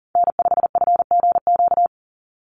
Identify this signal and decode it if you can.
Morse code, heard as a keyed tone, about 40 wpm, 700 Hz. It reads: N5FGQ
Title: USA Call Signs 40wpm